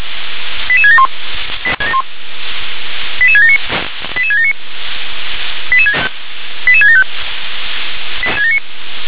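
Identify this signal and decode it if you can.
Five-tone selective call (ZVEI), heard as quick runs of five tones, tones of about 70 ms. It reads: 88671 88679 88679 88675
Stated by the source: Five-tone ZVEI selective call received on SDR.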